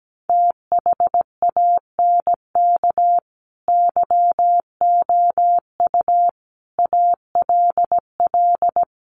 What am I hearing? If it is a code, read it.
Morse code, heard as a keyed tone, about 17 wpm, 700 Hz. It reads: THANK YOU ALL
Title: Sets of 5 Words Encore 17wpm